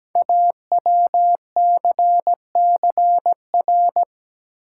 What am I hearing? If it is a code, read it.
Morse code, heard as a keyed tone, about 17 wpm, 700 Hz. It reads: AWCCR
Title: Code Groups - 5 Letters - 17wpm